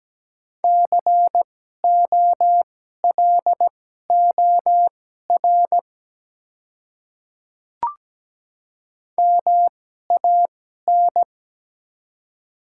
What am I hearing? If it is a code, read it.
Morse code, heard as a keyed tone, about 17 wpm, spaced as for 12 wpm, 700 Hz. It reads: COLOR MAN
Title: 2x Character Spacing - Top 300 Words - 17wpm